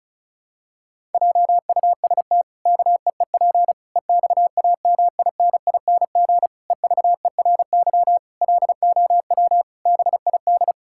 Morse code, heard as a keyed tone, about 36 wpm, 700 Hz. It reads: JUST KEEP EXAMINING EVERY LOW BID
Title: Pangram Challenge v2 - 35wpm